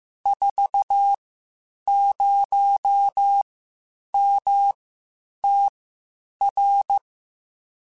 Morse code, heard as a keyed tone, about 15 wpm, spaced as for 8 wpm, 800 Hz. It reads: 40MTR